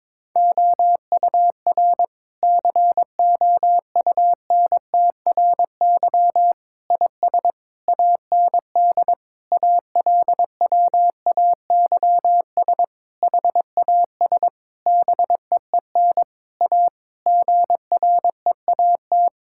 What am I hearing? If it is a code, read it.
Morse code, heard as a keyed tone, about 21 wpm, 700 Hz. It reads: OUR COUNTRY IS AND ALWAYS HAS BEEN A GREAT
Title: Sentences from Top 1000 Words - Review Part 6 - 22wpm